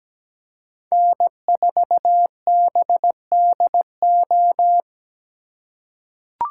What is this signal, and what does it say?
Morse code, heard as a keyed tone, about 17 wpm, 700 Hz. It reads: N4BDO
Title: USA Call Signs - 5 Characters - 3x Character Spacing - 17wpm